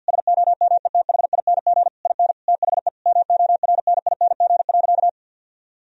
Morse code, heard as a keyed tone, about 50 wpm, 700 Hz. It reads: SOMETHING IN THE MORNING.
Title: Sentences from Top 1000 Words - Review Part 4 - 50wpm